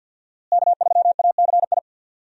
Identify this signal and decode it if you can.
Morse code, heard as a keyed tone, about 50 wpm, 700 Hz. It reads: K3ACI
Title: USA Call Signs - 5 Characters - Repeated 2x - 50wpm